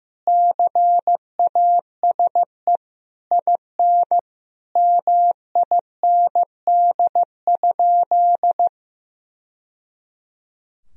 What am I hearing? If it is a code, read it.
Morse code, heard as a keyed tone, about 15 wpm, 700 Hz. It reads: CASE IN MIND?
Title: Sentences from Top 1000 Words - Review Part 4 - 15wpm